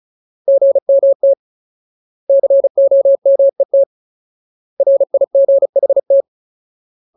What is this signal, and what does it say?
Morse code, heard as a keyed tone, about 36 wpm, 550 Hz. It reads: GMT COMET RIGHT
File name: Ultimate Word List / 35WPM Ultimate